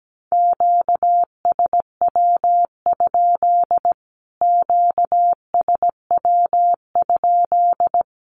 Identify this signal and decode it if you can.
Morse code, heard as a keyed tone, about 17 wpm, 700 Hz. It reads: QSW? QSW?
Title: Q-Codes Repeated 3x 17wpm